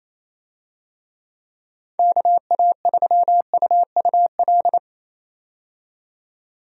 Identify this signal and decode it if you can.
Morse code, heard as a keyed tone, about 28 wpm, 700 Hz. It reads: KA3UUL
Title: USA Call Signs - 6 Characters - 28wpm